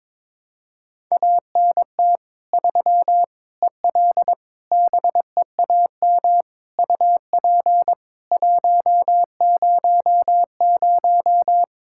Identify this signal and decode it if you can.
Morse code, heard as a keyed tone, about 21 wpm, 700 Hz. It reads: ANT 3 EL BEAM UP 100
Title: K1USN SST QSO Practice 22wpm